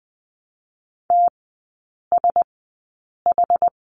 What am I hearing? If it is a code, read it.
Morse code, heard as a keyed tone, about 20 wpm, spaced as for 8 wpm, 700 Hz. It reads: TSH
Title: 074 Lesson 13e - Letter L - Three Characters 20wpm